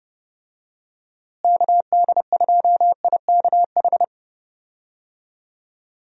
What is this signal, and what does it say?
Morse code, heard as a keyed tone, about 30 wpm, 700 Hz. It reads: KD2IKH
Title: USA Call Signs - 6 Characters - Repeated 2x - 30wpm